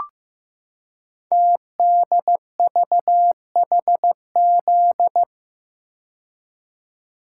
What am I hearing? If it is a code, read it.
Morse code, heard as a keyed tone, about 15 wpm, 700 Hz. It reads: TDVHZ
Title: Code Groups - 5 Letters - 15wpm